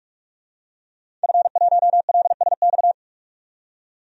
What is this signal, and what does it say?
Morse code, heard as a keyed tone, about 47 wpm, 700 Hz. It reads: F1LSX
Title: DXCC-100 Call Signs - 5 Characters - 45wpm